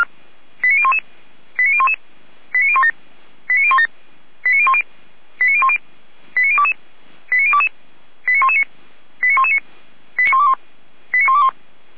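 Five-tone selective call (ZVEI), heard as quick runs of five tones, tones of about 70 ms. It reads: 89011 89011 89017 89017 89010 89010 89022 89022 89119 89119 89121 89121
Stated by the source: ZVEI five-tone selective call received on an SDR, from feuerwehr-orlamuende fire brigade server.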